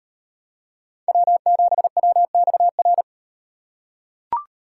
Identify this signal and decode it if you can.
Morse code, heard as a keyed tone, about 36 wpm, 700 Hz. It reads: W7WXR